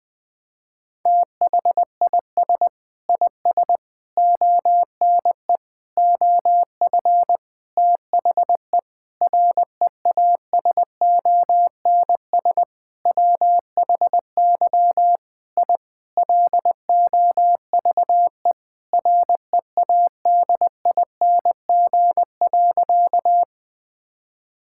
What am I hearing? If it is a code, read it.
Morse code, heard as a keyed tone, about 21 wpm, 700 Hz. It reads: THIS IS ONE OF THE REASONS WHY I LOVE READING.
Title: Sentences with 7-Word Phrases - 20wpm